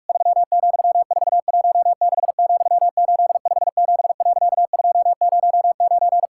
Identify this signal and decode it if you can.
Morse code, heard as a keyed tone, about 46 wpm, 700 Hz. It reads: 3,416,857.209
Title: Pangram Challenge v2 - 45wpm